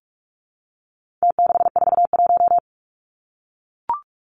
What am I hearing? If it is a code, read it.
Morse code, heard as a keyed tone, about 44 wpm, 700 Hz. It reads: T641
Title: Numbers - Contest Style - 4-Digit 45wpm